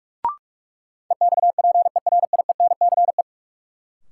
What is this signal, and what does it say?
Morse code, heard as a keyed tone, about 46 wpm, 700 Hz. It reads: EXPERIENCE
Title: Sentences from Top 600 Words - New Word Drill - 45wpm